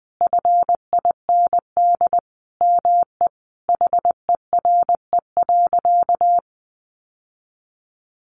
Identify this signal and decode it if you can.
Morse code, heard as a keyed tone, about 20 wpm, 700 Hz. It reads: FIND ME HERE.
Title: Sentences from Top 1200 Words 20wpm